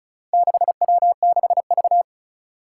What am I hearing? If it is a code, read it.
Morse code, heard as a keyed tone, about 36 wpm, 700 Hz. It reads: 6W6V
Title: Four Challenging Letter-Number Combinations 35wpm